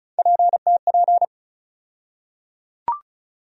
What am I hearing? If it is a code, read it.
Morse code, heard as a keyed tone, about 36 wpm, 700 Hz. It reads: PTP
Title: USA Call Sign Postfixes - 3 Letters 35wpm